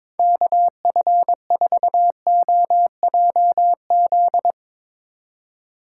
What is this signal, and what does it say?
Morse code, heard as a keyed tone, about 24 wpm, 700 Hz. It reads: KF4OJZ
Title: Speed Racing - USA Call Signs 22wpm